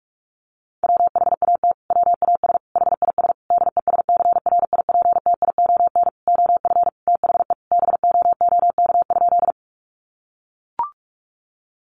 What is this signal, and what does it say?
Morse code, heard as a keyed tone, about 44 wpm, 700 Hz. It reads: WHAT WAS HIS DESCRIPTION OF THE BOOK?
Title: Sentences from Top 1300 Words 45wpm